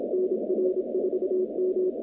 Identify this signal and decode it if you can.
Morse code, heard as a keyed tone, about 27 wpm, 370 Hz, with noise. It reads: NR4M
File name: 2019-cw-wpx-kt5j-k5pi